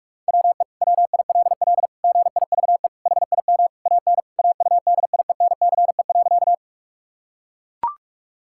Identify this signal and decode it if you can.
Morse code, heard as a keyed tone, about 43 wpm, 700 Hz. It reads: WE WILL GIVE HIM AN AUDIENCE.
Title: Sentences from Top 1000 Words - Review Part 5 - 45wpm